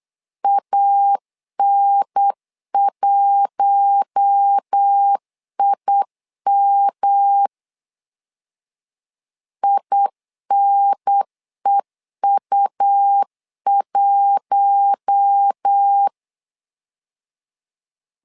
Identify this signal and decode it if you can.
Morse code, heard as a keyed tone, about 8 wpm, 800 Hz. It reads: AN1IM INEU1
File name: LEZIONI / 4 Q1 8wpm 3035